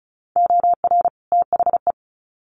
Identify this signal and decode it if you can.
Morse code, heard as a keyed tone, about 34 wpm, 700 Hz. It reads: OR THE
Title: Sets of 2 Words 35wpm